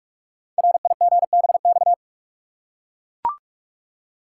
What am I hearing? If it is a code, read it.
Morse code, heard as a keyed tone, about 44 wpm, 700 Hz. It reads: RIGBX
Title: Code Groups - 5 Letters - 45wpm